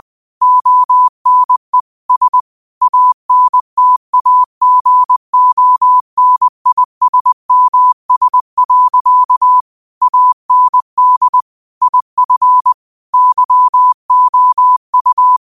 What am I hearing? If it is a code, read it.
Morse code, heard as a keyed tone, about 20 wpm, 1000 Hz. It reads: ONE S ANTAGONISMS. AND IF YOU